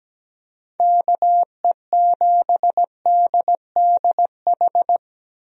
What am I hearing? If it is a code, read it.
Morse code, heard as a keyed tone, about 17 wpm, 700 Hz. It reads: KE7DDH
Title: USA Call Signs - 6 Characters - Repeated 3x - 17wpm